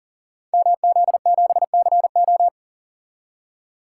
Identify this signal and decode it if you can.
Morse code, heard as a keyed tone, about 40 wpm, 690 Hz. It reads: MZ7CO
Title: Code Groups - 5 Letter:Numbers - 40wpm